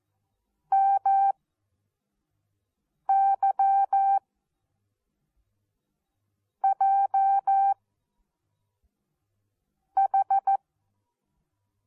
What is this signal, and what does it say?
Morse code, heard as a keyed tone, about 14 wpm, spaced as for 4 wpm, 780 Hz. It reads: MY J H
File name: Onlinelehrgang / morsen / DJ4UF Morsen14